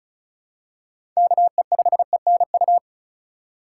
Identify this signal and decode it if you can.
Morse code, heard as a keyed tone, about 36 wpm, 700 Hz. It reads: KE5ENU
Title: USA Call Signs - 6 Characters - 2x Character Spacing - 35wpm